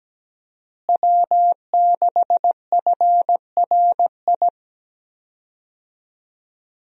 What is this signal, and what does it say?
Morse code, heard as a keyed tone, about 17 wpm, 700 Hz. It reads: W6FRI
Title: USA Call Signs - 5 Characters - Repeated 3x - 17wpm